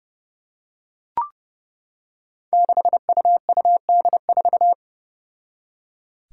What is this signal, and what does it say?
Morse code, heard as a keyed tone, about 28 wpm, 700 Hz. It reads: 6UUD4
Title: Five Challenging Letter-Number Combinations 30wpm